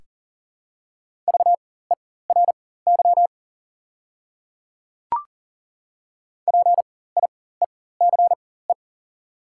Morse code, heard as a keyed tone, about 40 wpm, spaced as for 19 wpm, 700 Hz. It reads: VERY PIECE
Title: 4x Character Spacing - Top 500 Words - 40wpm